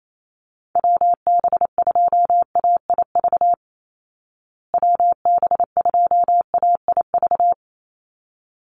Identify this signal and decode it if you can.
Morse code, heard as a keyed tone, about 28 wpm, 700 Hz. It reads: WB2AIV WB2AIV
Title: Call Signs Repeated 3x - 28wpm